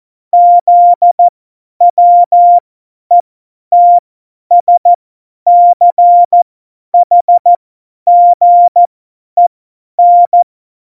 Morse code, heard as a keyed tone, about 14 wpm, spaced as for 10 wpm, 700 Hz. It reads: ZWETSCHGEN
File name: Morsekurs / Witze / Witz 23. April 2019, 50 F70